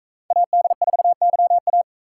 Morse code, heard as a keyed tone, about 42 wpm, 700 Hz. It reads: AD4YA